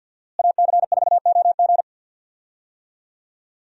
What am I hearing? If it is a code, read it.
Morse code, heard as a keyed tone, about 51 wpm, 700 Hz. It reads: AC4OG
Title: USA Call Signs - 5 Characters - Repeated 3x - 50wpm